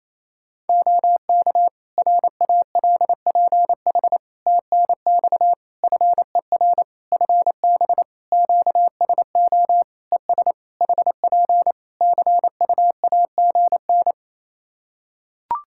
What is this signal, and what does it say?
Morse code, heard as a keyed tone, about 28 wpm, 700 Hz. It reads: OK RALPH TNX FER FB QSO ES HP CUAGN
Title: Practice QSO - Exchange 4 - Ending - 28wpm